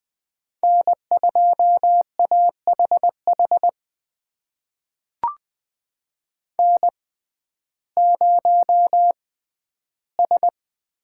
Morse code, heard as a keyed tone, about 20 wpm, spaced as for 12 wpm, 700 Hz. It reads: N2AHH N0S
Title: USA Call Signs - 5 Characters - 6x Character Spacing - 20wpm